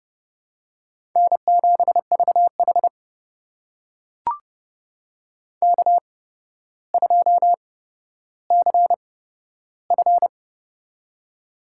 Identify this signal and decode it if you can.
Morse code, heard as a keyed tone, about 29 wpm, spaced as for 13 wpm, 700 Hz. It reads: N7VH K2CF
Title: USA Call Signs - 4 Characters - 8x Character Spacing - 30wpm